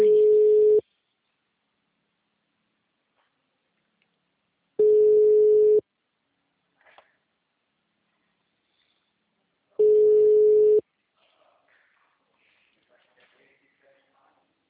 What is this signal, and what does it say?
Ringback tone (Europe, 425 Hz), with speech.
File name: out- -2003- - - .58